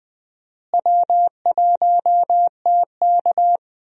Morse code, heard as a keyed tone, about 21 wpm, 700 Hz. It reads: W1TK
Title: USA Call Signs - 4 Characters - Repeated 2x - 20wpm